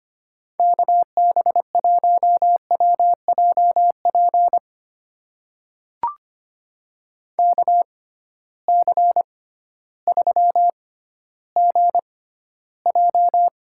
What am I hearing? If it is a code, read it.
Morse code, heard as a keyed tone, about 25 wpm, spaced as for 14 wpm, 700 Hz. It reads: KB1WJP KC3GJ
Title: USA Call Signs - 6 Characters - 6x Character Spacing - 25wpm